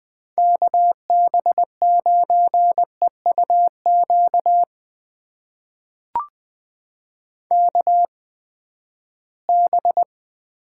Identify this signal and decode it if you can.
Morse code, heard as a keyed tone, about 20 wpm, spaced as for 14 wpm, 700 Hz. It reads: KB9EUQ KB
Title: USA Call Signs - 6 Characters - 8x Character Spacing - 20wpm